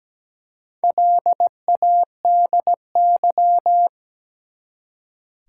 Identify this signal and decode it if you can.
Morse code, heard as a keyed tone, about 17 wpm, 700 Hz. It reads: LADY
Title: Four Letter Words - 17wpm